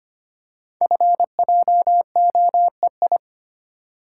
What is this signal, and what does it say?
Morse code, heard as a keyed tone, about 25 wpm, 690 Hz. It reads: FJOEI
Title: Code Groups - 5 Letters - 25wpm